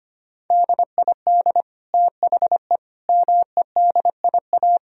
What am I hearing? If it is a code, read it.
Morse code, heard as a keyed tone, about 25 wpm, 700 Hz. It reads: DID THE MEDIA
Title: Sentences from Top 1000 Words - Review Part 4 - 25wpm